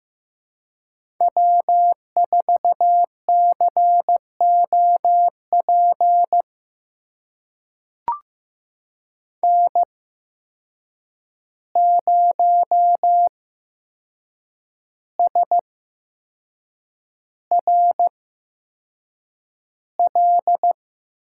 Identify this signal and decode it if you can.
Morse code, heard as a keyed tone, about 15 wpm, spaced as for 6 wpm, 700 Hz. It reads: W4COP N0SRL
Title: USA Call Signs - 5 Characters - 8x Character Spacing - 15wpm